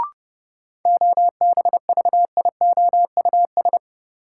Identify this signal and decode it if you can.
Morse code, heard as a keyed tone, about 30 wpm, 700 Hz. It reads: OBVIOUS
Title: Sentences from Top 1600 Words - New Word Drill - 30wpm